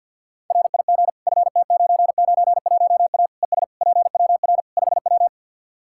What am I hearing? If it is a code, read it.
Morse code, heard as a keyed tone, about 47 wpm, 700 Hz. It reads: RIG FT991A ES PWR 5W
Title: Practice QSO - Exchange 2 - Info - 50wpm